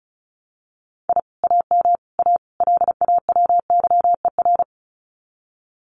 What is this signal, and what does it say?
Morse code, heard as a keyed tone, about 34 wpm, 700 Hz. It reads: I AM A LAWYER
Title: Common QSO Phrases Encore 35wpm